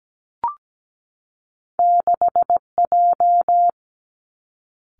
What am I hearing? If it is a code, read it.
Morse code, heard as a keyed tone, about 17 wpm, 700 Hz. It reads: 6J
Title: Two Challenging Letter-Number Combinations 17wpm